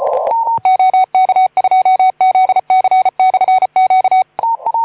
Morse code, heard as a keyed tone, about 34 wpm, 2250 Hz. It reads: OK2ZC/Q
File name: Archiv / CUC181 / OK2ZC CUC 181